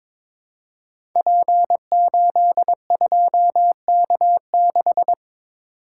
Morse code, heard as a keyed tone, about 22 wpm, 700 Hz. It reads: P82K6
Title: Code Groups - 5 Letter:Numbers - 22wpm